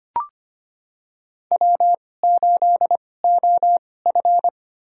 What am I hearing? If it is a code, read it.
Morse code, heard as a keyed tone, about 25 wpm, spaced as for 18 wpm, 700 Hz. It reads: W8OF
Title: USA Call Signs - 2x Character Spacing - 25wpm